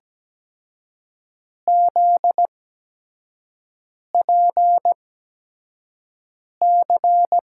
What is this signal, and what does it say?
Morse code, heard as a keyed tone, about 17 wpm, spaced as for 5 wpm, 700 Hz. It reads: ZPC
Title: USA Call Signs - 6 Characters - 8x Character Spacing - 17wpm